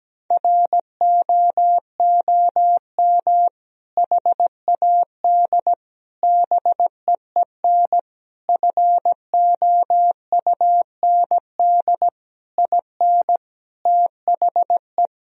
Morse code, heard as a keyed tone, about 17 wpm, 700 Hz. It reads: ROOM HAD BEEN FOUND IN THE
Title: Sentences from Top 1000 Words - Review Part 3 - 17wpm